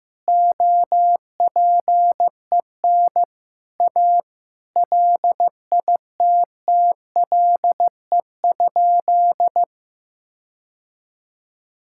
Morse code, heard as a keyed tone, about 15 wpm, 700 Hz. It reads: OPEN A LITTLE?
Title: Sentences from Top 1600 Words 15wpm